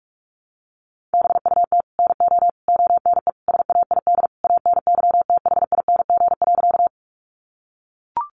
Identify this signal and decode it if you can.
Morse code, heard as a keyed tone, about 44 wpm, 700 Hz. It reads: BUT NO ONE SAID ANYTHING.
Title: Sentences from Top 1000 Words 45wpm